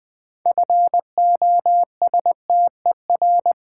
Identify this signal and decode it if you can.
Morse code, heard as a keyed tone, about 20 wpm, 700 Hz. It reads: FOSTER
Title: Build A Word - 2 Letters At A Time - 6-Letter Words - 20wpm